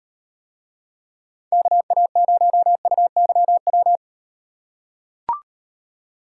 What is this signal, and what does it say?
Morse code, heard as a keyed tone, about 38 wpm, 700 Hz. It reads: KA0UYW